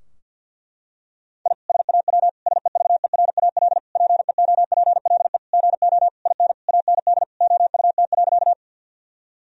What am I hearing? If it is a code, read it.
Morse code, heard as a keyed tone, about 46 wpm, 700 Hz. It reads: I SAW SEVERAL PEOPLE GO IN AND OUT.
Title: Sentences from Top 1000 Words - Review Part 2 - 50wpm